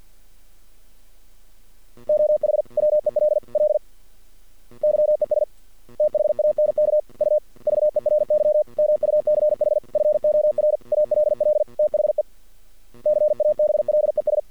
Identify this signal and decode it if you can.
Morse code, heard as a keyed tone, about 48 wpm, 600 Hz. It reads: OH WELL YES THANK U VERY MUCH FOR ALL THE CABLES
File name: archived / CRSnet